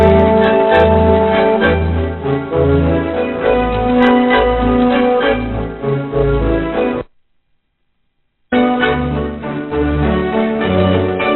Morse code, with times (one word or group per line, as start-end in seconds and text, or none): none